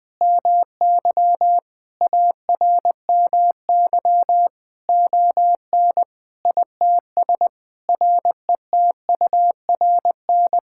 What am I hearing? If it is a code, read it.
Morse code, heard as a keyed tone, about 20 wpm, 700 Hz. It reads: MY ARMY ON ITS RETURN
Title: Sentences from Top 1000 Words - Review Part 5 - 20wpm